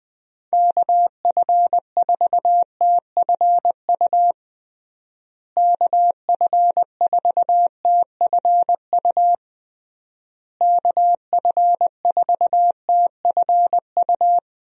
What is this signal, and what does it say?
Morse code, heard as a keyed tone, about 21 wpm, 700 Hz. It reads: KF4TFU KF4TFU KF4TFU
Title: USA Call Signs - 6 Characters - Repeated 3x - 20wpm